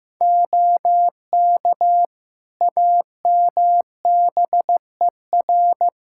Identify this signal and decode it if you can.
Morse code, heard as a keyed tone, about 14 wpm, 700 Hz. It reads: OK AMBER
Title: Practice QSO - Exchange 3 - More Info - 15wpm